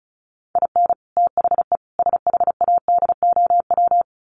Morse code, heard as a keyed tone, about 34 wpm, 700 Hz. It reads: IN THE SHADOW
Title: Sets of 5 Words Encore 35wpm